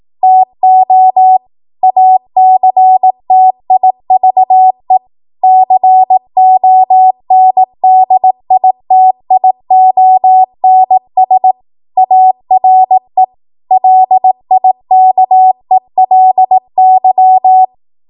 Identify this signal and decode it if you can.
Morse code, heard as a keyed tone, about 18 wpm, 750 Hz. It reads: TO ACTIVE CONDITIONS ARE LIKELY
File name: Morse / 18WPM